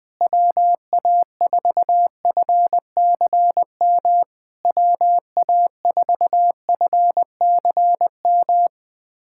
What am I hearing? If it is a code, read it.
Morse code, heard as a keyed tone, about 21 wpm, 700 Hz. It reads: WA4FCM WA4FCM
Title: SOTA QSO Practice 20wpm